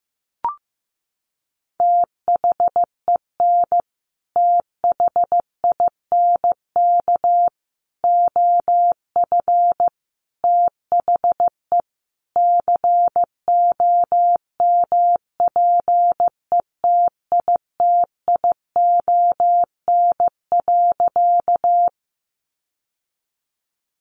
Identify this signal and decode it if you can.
Morse code, heard as a keyed tone, about 14 wpm, 700 Hz. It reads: THEN THINK OF THE COMPETITION.
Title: Sentences from Top 1000 Words 15wpm